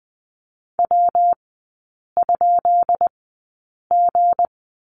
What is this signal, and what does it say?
Morse code, heard as a keyed tone, about 20 wpm, spaced as for 8 wpm, 700 Hz. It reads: W?G
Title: 212 Lesson 34e - Number 8 - Three Characters 20wpm